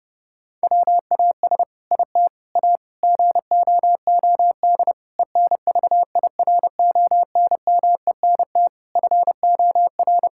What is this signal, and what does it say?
Morse code, heard as a keyed tone, about 32 wpm, 700 Hz. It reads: WAS IT A GOOD ENVIRONMENT FOR
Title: Sentences from Top 1000 Words - Review Part 5 - 30wpm